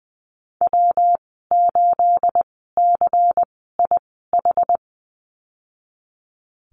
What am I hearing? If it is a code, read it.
Morse code, heard as a keyed tone, about 20 wpm, 700 Hz. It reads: W 8 C I H
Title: 238 Lesson 38g - Letter q - Call Signs 20wpm